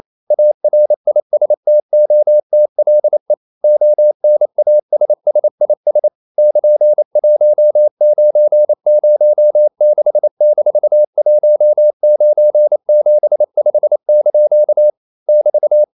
Morse code, heard as a keyed tone, about 28 wpm, 600 Hz. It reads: ARISTOTLE ONASSIS <KN>1906-1975) <BT>
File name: CW / Practice / EnglishQuotations0228